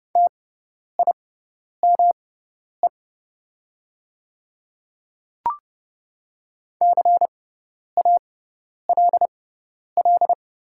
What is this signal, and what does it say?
Morse code, heard as a keyed tone, about 30 wpm, spaced as for 10 wpm, 700 Hz. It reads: TIME CALL
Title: 6x Character Spacing - Top 200 Words - 30wpm